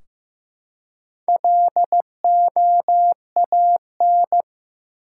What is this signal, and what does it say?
Morse code, heard as a keyed tone, about 15 wpm, 700 Hz. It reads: LOAN
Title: Four Letter Words - 15wpm